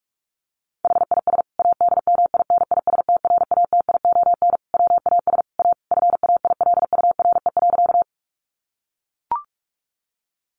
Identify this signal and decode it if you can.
Morse code, heard as a keyed tone, about 44 wpm, 700 Hz. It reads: HIS ADMINISTRATION WAS A FAILURE.
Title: Sentences from Top 1300 Words 45wpm